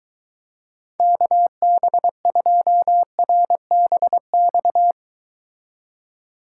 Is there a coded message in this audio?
Yes: Morse code from 1.0 to 4.9 s, heard as a keyed tone, about 22 wpm, 700 Hz. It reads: KB2RBX